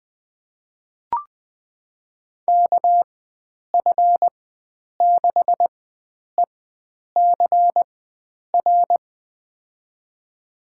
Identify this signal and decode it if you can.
Morse code, heard as a keyed tone, about 20 wpm, spaced as for 9 wpm, 700 Hz. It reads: KF6ECR
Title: USA Call Signs - 6 Characters - 4x Character Spacing - 20wpm